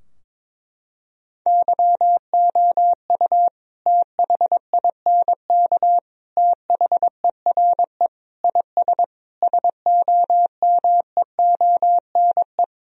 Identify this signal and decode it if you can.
Morse code, heard as a keyed tone, about 22 wpm, 700 Hz. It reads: YOU THINK THERE IS SOMEONE
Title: Sentences from Top 1000 Words - Review Part 4 - 22wpm